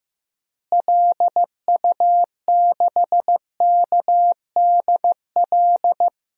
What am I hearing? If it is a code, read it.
Morse code, heard as a keyed tone, about 15 wpm, 700 Hz. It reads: LU6KDL
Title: DXCC-100 Call Signs - 6 Characters - Repeated 3x - 15wpm